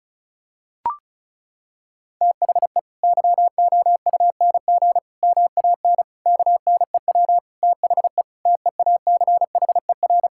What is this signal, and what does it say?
Morse code, heard as a keyed tone, about 36 wpm, 700 Hz. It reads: THE YOUNG MAN KNEW THE TEACHER
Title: Sentences from Top 1000 Words - Review Part 6 - 35wpm